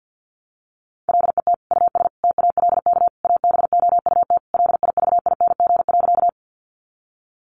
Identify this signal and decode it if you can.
Morse code, heard as a keyed tone, about 50 wpm, 700 Hz. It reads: LET US TALK ABOUT LIVING.
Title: Sentences from Top 1100 Words 50wpm